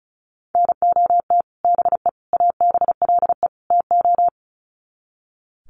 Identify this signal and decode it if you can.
Morse code, heard as a keyed tone, about 34 wpm, 700 Hz. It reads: NOT BE ABLE TO
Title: Sets of 4 Words 35wpm